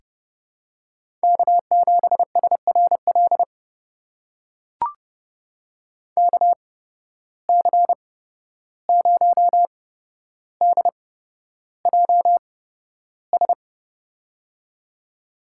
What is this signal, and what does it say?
Morse code, heard as a keyed tone, about 30 wpm, spaced as for 12 wpm, 700 Hz. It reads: K7SRL KC0DJS
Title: USA Call Signs - 8x Character Spacing - 30wpm